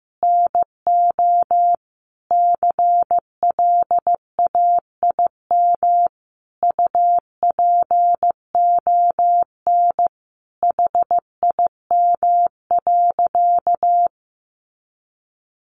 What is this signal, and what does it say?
Morse code, heard as a keyed tone, about 15 wpm, 700 Hz. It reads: NO CLAIM UPON HIM.
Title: Sentences from Top 1500 Words 15wpm